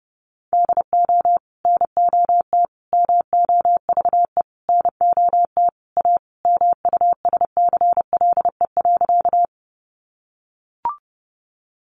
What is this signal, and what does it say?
Morse code, heard as a keyed tone, about 28 wpm, 700 Hz. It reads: DO NOT MOVE NOT A MUSCLE.
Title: Sentences from Top 1300 Words 30wpm